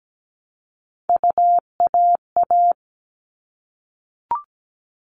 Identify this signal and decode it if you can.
Morse code, heard as a keyed tone, about 17 wpm, 700 Hz. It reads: UAA
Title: All Cut Numbers - 3-Digit 17wpm